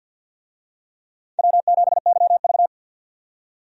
Morse code, heard as a keyed tone, about 51 wpm, 700 Hz. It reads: W7YV
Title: USA Call Signs - 4 Characters - 50wpm